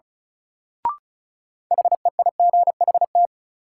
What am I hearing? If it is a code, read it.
Morse code, heard as a keyed tone, about 36 wpm, 700 Hz. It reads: HEIGHT
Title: Sentences from Top 1600 Words - New Word Drill Alt - 35wpm